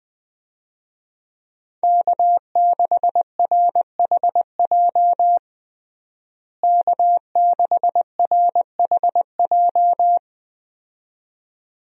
Morse code, heard as a keyed tone, about 20 wpm, 700 Hz. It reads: K6RHJ K6RHJ
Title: USA Call Signs - 5 Characters - Repeated 2x - 20wpm